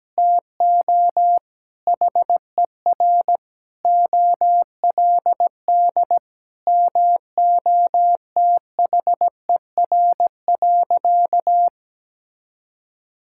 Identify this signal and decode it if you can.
Morse code, heard as a keyed tone, about 17 wpm, 700 Hz. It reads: TO HER OLD MOTHER.
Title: Sentences from Top 1000 Words - Review Part 5 - 17wpm